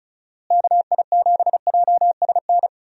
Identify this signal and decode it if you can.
Morse code, heard as a keyed tone, about 36 wpm, 700 Hz. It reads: KI7JSN
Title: USA Call Signs - 6 Characters - Repeated 3x - 35wpm